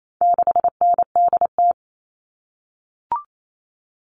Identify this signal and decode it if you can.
Morse code, heard as a keyed tone, about 28 wpm, 700 Hz. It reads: 6NDT
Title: All Cut Numbers - 4-Digit 28wpm